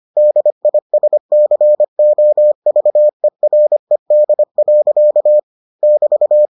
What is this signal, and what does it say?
Morse code, heard as a keyed tone, about 25 wpm, 600 Hz. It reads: DISCOVERED. <BT>